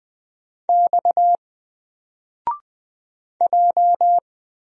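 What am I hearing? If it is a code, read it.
Morse code, heard as a keyed tone, about 20 wpm, 700 Hz. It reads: X J
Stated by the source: Single letter and number Morse drill at 20 wpm, from N4LCD.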